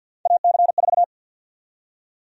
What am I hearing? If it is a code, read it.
Morse code, heard as a keyed tone, about 49 wpm, 700 Hz. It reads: AC4
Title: USA Call Sign Prefixes - 50wpm